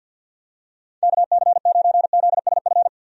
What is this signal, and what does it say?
Morse code, heard as a keyed tone, about 48 wpm, 700 Hz. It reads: KC9ZSF
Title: USA Call Signs - 6 Characters - 50wpm